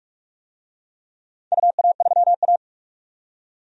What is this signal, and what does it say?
Morse code, heard as a keyed tone, about 42 wpm, 700 Hz. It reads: UA3A
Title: DXCC-100 Call Signs - 4 Characters - 45wpm